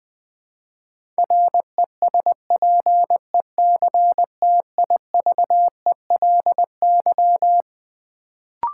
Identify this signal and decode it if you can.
Morse code, heard as a keyed tone, about 21 wpm, 700 Hz. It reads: RESPECTIVELY
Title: Build A Word - 4 Letters At A Time - 12-Letter Words - 20wpm